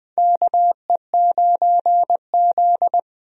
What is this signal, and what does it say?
Morse code, heard as a keyed tone, about 21 wpm, 700 Hz. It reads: KE9Z
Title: USA Call Signs - 4 Characters - Repeated 2x - 20wpm